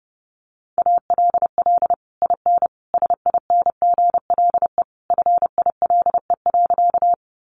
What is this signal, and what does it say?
Morse code, heard as a keyed tone, about 30 wpm, 700 Hz. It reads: ALL IN SINGLE FILE.
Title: Sentences from Top 1200 Words 30wpm